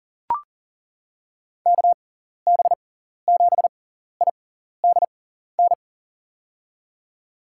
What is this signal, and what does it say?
Morse code, heard as a keyed tone, about 40 wpm, spaced as for 14 wpm, 700 Hz. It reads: KB7IDN
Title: USA Call Signs - 6 Characters - 6x Character Spacing - 40wpm